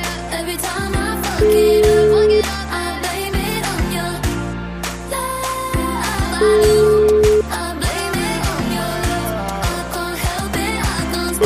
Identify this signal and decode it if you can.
Ringback tone (Europe, 425 Hz), with noise.